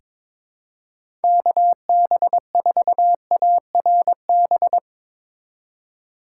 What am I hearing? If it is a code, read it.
Morse code, heard as a keyed tone, about 22 wpm, 700 Hz. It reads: KB4ARB
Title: USA Call Signs - 4x Character Spacing - 22wpm